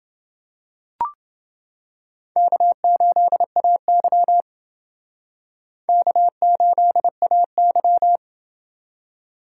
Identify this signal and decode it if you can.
Morse code, heard as a keyed tone, about 29 wpm, 700 Hz. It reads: K8AY K8AY